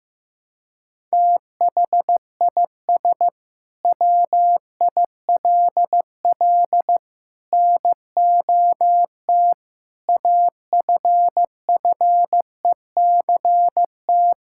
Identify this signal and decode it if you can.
Morse code, heard as a keyed tone, about 14 wpm, 700 Hz. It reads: THIS WILL NOT AFFECT
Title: Sentences from Top 1000 Words - Review Part 6 - 15wpm